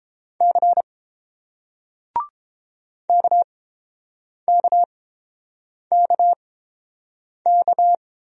Morse code, heard as a keyed tone, about 30 wpm, spaced as for 7 wpm, 700 Hz. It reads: C KKKK